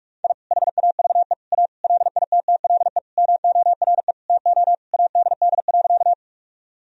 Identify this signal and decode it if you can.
Morse code, heard as a keyed tone, about 45 wpm, 700 Hz. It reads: I HAVE A LITTLE MORE TO ADD.
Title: Sentences from Top 1000 Words - Review Part 1 - 45wpm